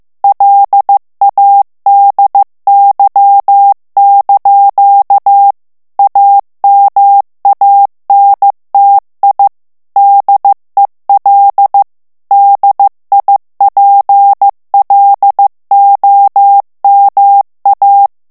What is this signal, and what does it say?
Morse code, heard as a keyed tone, about 15 wpm, 800 Hz. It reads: LADY) AMANTI DEL DIPLOMA
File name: ITALIEN texte clair 1 vitesse 14 mots minute